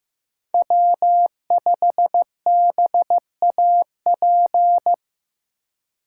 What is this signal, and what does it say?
Morse code, heard as a keyed tone, about 15 wpm, 700 Hz. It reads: W5BAP
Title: USA Call Signs - 5 Characters - 4x Character Spacing - 15wpm